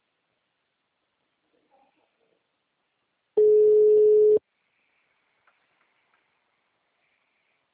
Ringback tone (Europe, 425 Hz).